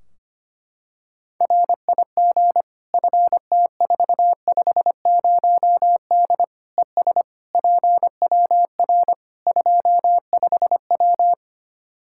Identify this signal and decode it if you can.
Morse code, heard as a keyed tone, about 25 wpm, 700 Hz. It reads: RIG FT450D ES PWR 25W
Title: Practice QSO - Exchange 2 - Info - 25wpm